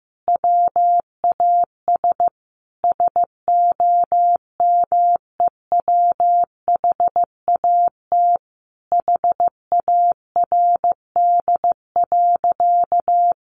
Morse code, heard as a keyed tone, about 15 wpm, 700 Hz. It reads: WAS SOMEWHAT HARD.
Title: Sentences from Top 1300 Words 15wpm